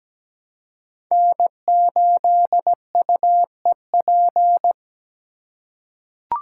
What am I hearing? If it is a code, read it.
Morse code, heard as a keyed tone, about 17 wpm, 700 Hz. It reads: N8UEP
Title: Code Groups - 5 Letter:Numbers - 17wpm